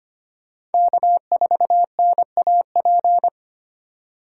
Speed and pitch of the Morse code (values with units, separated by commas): 33 wpm, 700 Hz